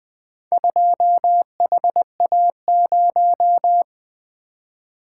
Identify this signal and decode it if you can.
Morse code, heard as a keyed tone, about 20 wpm, 700 Hz. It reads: 2HA0
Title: Four Letter-Number Combinations - 20wpm